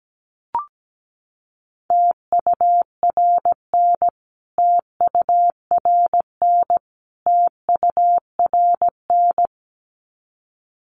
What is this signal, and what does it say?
Morse code, heard as a keyed tone, about 17 wpm, 700 Hz. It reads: TURN TURN TURN
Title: Top 500 Words Repeated 17wpm